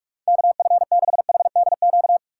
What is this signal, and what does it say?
Morse code, heard as a keyed tone, about 46 wpm, 700 Hz. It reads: KF6HDQ